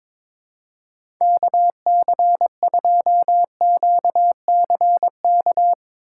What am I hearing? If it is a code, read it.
Morse code, heard as a keyed tone, about 22 wpm, 700 Hz. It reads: KC2QCK
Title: USA Call Signs - 6 Characters - 3x Character Spacing - 22wpm